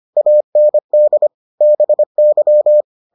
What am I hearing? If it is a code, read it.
Morse code, heard as a keyed tone, about 25 wpm, 600 Hz. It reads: AND BY